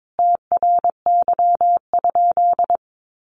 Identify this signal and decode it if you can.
Morse code, heard as a keyed tone, about 22 wpm, 700 Hz. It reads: TRY?
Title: Sentences from Top 200 Words 22wpm